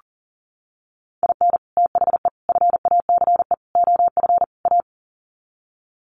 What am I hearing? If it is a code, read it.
Morse code, heard as a keyed tone, about 40 wpm, 700 Hz. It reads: IN THE FACE OF A
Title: Sets of 5 Words Encore 40wpm